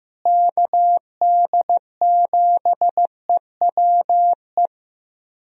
Morse code, heard as a keyed tone, about 15 wpm, 700 Hz. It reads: KD7EWE
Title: USA Call Signs - 6 Characters - Repeated 3x - 15wpm